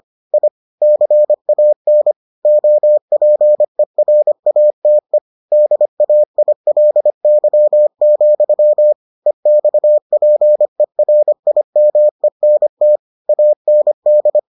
Morse code, heard as a keyed tone, about 25 wpm, 600 Hz. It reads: I CAN OPERATE DAILY, EXPERIMENT AND